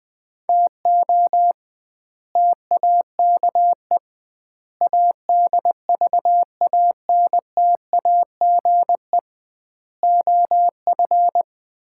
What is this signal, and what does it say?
Morse code, heard as a keyed tone, about 20 wpm, 700 Hz. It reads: TO TAKE ADVANTAGE OF
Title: 2x Word-Spacing - Sets of 5 Words - 20wpm